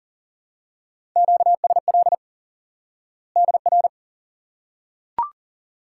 Morse code, heard as a keyed tone, about 40 wpm, 700 Hz. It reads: QSL DR
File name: IQE Course 3 - Lesson 89 - YAGI - Two Elements - 40wpm